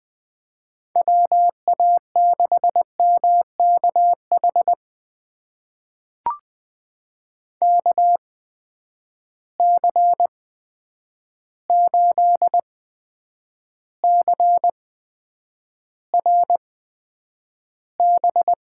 Morse code, heard as a keyed tone, about 20 wpm, spaced as for 9 wpm, 700 Hz. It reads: WA6MKH KC8CRB
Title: USA Call Signs - 6 Characters - 8x Character Spacing - 20wpm